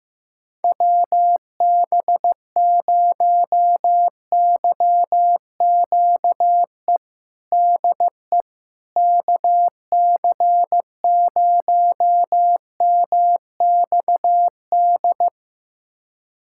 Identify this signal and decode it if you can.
Morse code, heard as a keyed tone, about 15 wpm, 700 Hz. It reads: WB0YQE DE KC0MXD
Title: USA Callsign DE Callsign - 6 Characters - 15wpm